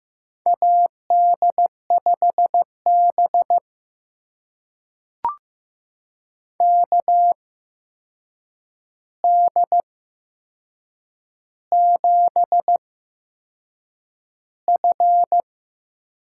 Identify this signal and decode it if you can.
Morse code, heard as a keyed tone, about 15 wpm, spaced as for 6 wpm, 700 Hz. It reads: AD5BK D7F
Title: USA Call Signs - 8x Character Spacing - 15wpm